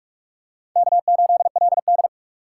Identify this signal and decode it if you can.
Morse code, heard as a keyed tone, about 45 wpm, 700 Hz. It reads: K8LD